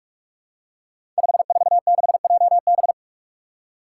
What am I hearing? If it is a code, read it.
Morse code, heard as a keyed tone, about 42 wpm, 700 Hz. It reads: 546JB
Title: Five Challenging Letter-Number Combinations 45wpm